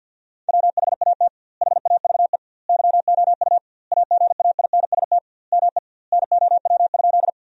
Morse code, heard as a keyed tone, about 50 wpm, 700 Hz. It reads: WHAT HAVE YOU AGAINST ME NOW?
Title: 2x Word-Spacing - Sentences from Top 400 Words - 50wpm